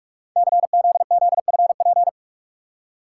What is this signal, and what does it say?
Morse code, heard as a keyed tone, about 44 wpm, 700 Hz. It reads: CZZFP
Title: Code Groups - 5 Letter:Numbers - 45wpm